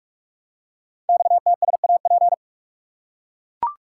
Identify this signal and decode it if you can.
Morse code, heard as a keyed tone, about 44 wpm, 700 Hz. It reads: XTSAP
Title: Code Groups - 5 Letters - 45wpm